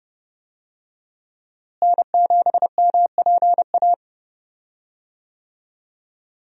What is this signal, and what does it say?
Morse code, heard as a keyed tone, about 32 wpm, 700 Hz. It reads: N7MPA
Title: USA Call Signs - 5 Characters - Repeated 3x - 30wpm